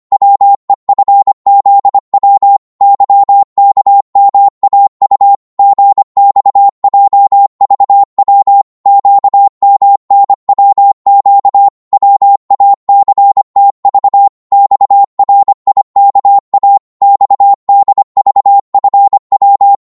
Morse code, heard as a keyed tone, about 24 wpm, 800 Hz. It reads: WEFZW YKMAU GXJVW QMNWQ WACTV XRIKA XDVFW
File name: Morse / ltr-25wpm